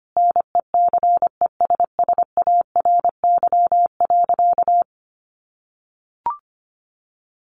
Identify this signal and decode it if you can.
Morse code, heard as a keyed tone, about 25 wpm, 700 Hz. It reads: NECESSARY.
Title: Sentences from Top 1400 Words 25wpm